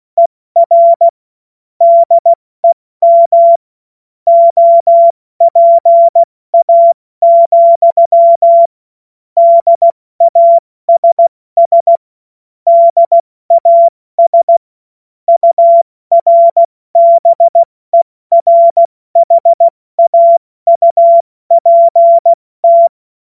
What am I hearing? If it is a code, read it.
Morse code, heard as a keyed tone, about 16 wpm, 680 Hz. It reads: ER DEM OPA, DASS DAS URBERHAUPT
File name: Morsekurs / Witze / Witz 19. Oktober 2021, 70 F80